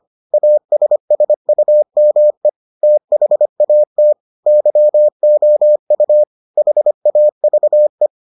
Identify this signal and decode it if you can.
Morse code, heard as a keyed tone, about 25 wpm, 600 Hz. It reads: ASSUME THAT YOU HAVE